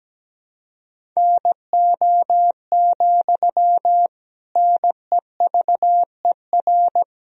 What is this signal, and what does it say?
Morse code, heard as a keyed tone, about 17 wpm, 700 Hz. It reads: NO, NEVER
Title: Sentences from Top 1000 Words - Review Part 1 - 17wpm